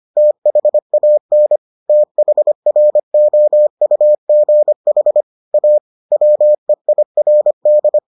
Morse code, heard as a keyed tone, about 25 wpm, 600 Hz. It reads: THAN THROUGH A WEIRD